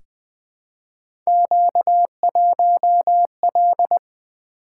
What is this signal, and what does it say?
Morse code, heard as a keyed tone, about 20 wpm, 700 Hz. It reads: Q1L
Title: Three Letter-Number Combinations - 20wpm